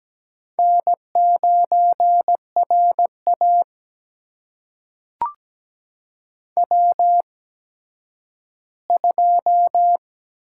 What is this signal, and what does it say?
Morse code, heard as a keyed tone, about 17 wpm, spaced as for 10 wpm, 700 Hz. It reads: N9RA W2
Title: USA Call Signs - 8x Character Spacing - 17wpm